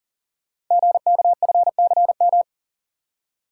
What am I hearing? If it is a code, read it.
Morse code, heard as a keyed tone, about 42 wpm, 700 Hz. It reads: GKFCM
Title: Code Groups - 5 Letters - 40wpm